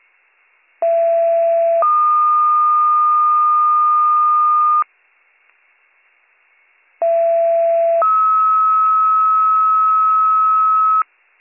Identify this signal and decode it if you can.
Two-tone page, heard as a one-second tone, then a three-second tone, about 670 then 1220 Hz, about 670 then 1290 Hz. It is heard from 0.8 to 11.0 s.